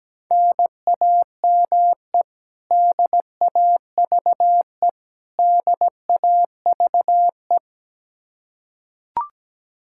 Morse code, heard as a keyed tone, about 17 wpm, 700 Hz. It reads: NAME DAVE DAVE
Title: Practice QSO - Exchange 1 - Beginning - 17wpm